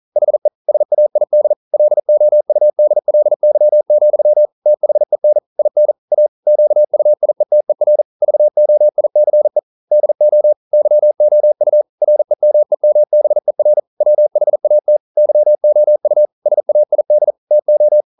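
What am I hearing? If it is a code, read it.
Morse code, heard as a keyed tone, about 42 wpm, 600 Hz. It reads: HE SAID LOUDLY, THEN IN A QUIETER VOICE DO YOU REMEMBER WHAT YOU SAID TO
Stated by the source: Morse code practice at 41 wpm, from CW med Gnister.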